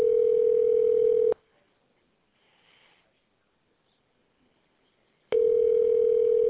Ringback tone (North America), 440 + 480 Hz.